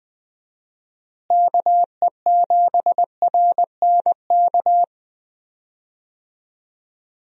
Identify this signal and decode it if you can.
Morse code, heard as a keyed tone, about 20 wpm, 700 Hz. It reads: KE7RNK
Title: USA Call Signs - 6 Characters - Repeated 2x - 20wpm